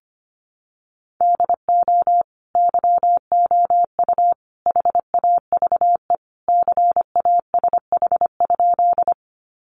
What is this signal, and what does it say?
Morse code, heard as a keyed tone, about 25 wpm, 700 Hz. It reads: DO YOU HAVE CASH?
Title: Sentences from Top 700 Words 25wpm